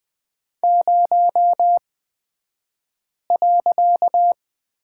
Morse code, heard as a keyed tone, about 20 wpm, spaced as for 5 wpm, 700 Hz. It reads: 0.
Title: 254 Lesson 41d - Period - Two Characters 20wpm